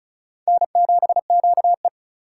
Morse code, heard as a keyed tone, about 34 wpm, 700 Hz. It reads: N7QE
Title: USA Call Signs - 3x Character Spacing - 35wpm